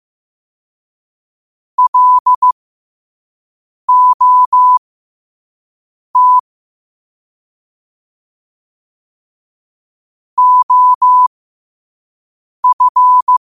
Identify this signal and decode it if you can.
Morse code, heard as a keyed tone, about 15 wpm, spaced as for 5 wpm, 1000 Hz. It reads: LOT OF
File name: qotdfiles / 5wpm